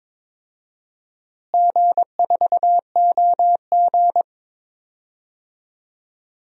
Morse code, heard as a keyed tone, about 22 wpm, 700 Hz. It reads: G4OG
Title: DXCC-100 Call Signs - 4 Characters - Repeated 3x - 22wpm